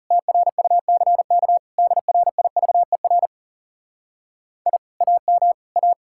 Morse code, heard as a keyed tone, about 37 wpm, 700 Hz. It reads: TRUCK DRIVER I AM A
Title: Speed-Racing Common QSO Phrases - 30wpm